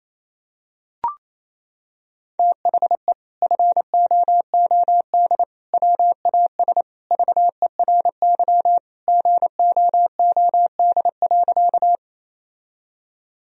Morse code, heard as a keyed tone, about 28 wpm, 700 Hz. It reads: THE FOOD WAS VERY GOOD.
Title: Sentences from Top 1000 Words - Review Part 1 - 28wpm